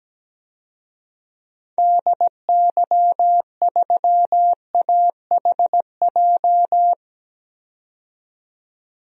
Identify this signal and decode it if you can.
Morse code, heard as a keyed tone, about 17 wpm, 700 Hz. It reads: DY3AHJ
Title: DXCC-100 Call Signs - 6 Characters - 17wpm